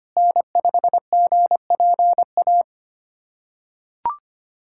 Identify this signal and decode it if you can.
Morse code, heard as a keyed tone, about 25 wpm, 700 Hz. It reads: N5GPA
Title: USA Callsign DE Callsign - 5 Characters - 25wpm